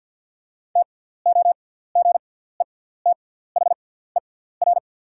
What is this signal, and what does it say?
Morse code, heard as a keyed tone, about 50 wpm, spaced as for 17 wpm, 690 Hz. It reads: TOGETHER
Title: 6x Character Spacing - Top 300 Words - 50wpm